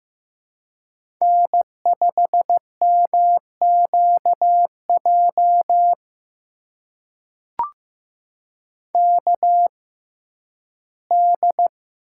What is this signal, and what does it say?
Morse code, heard as a keyed tone, about 15 wpm, spaced as for 11 wpm, 700 Hz. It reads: N5MQJ KD
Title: USA Call Signs - 6x Character Spacing - 15wpm